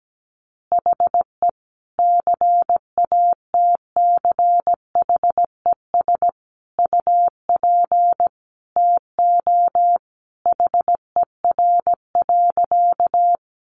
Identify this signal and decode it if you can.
Morse code, heard as a keyed tone, about 17 wpm, 700 Hz. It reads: HE CATCHES UP TO HER.
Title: Additional Easy Phrases 17wpm